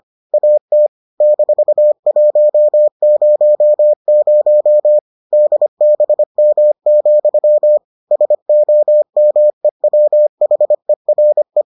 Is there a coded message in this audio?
Yes: Morse code from 0.3 to 11.6 s, heard as a keyed tone, about 25 wpm, 600 Hz. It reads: AT -100 DBM, SOMEWHERE